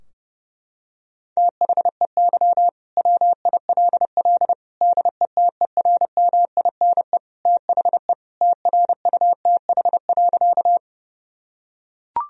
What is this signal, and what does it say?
Morse code, heard as a keyed tone, about 30 wpm, 700 Hz. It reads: THEY WILL DETERMINE THE TRUTH.
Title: Sentences from Top 1000 Words - Review Part 4 - 30wpm